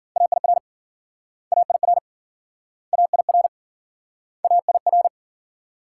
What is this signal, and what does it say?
Morse code, heard as a keyed tone, about 58 wpm, 700 Hz. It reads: AIR AIR AIR AIR